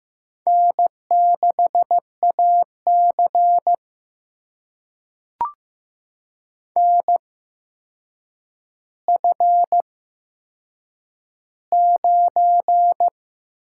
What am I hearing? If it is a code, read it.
Morse code, heard as a keyed tone, about 15 wpm, spaced as for 7 wpm, 700 Hz. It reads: N6AC NF9
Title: USA Call Signs - 4 Characters - 8x Character Spacing - 15wpm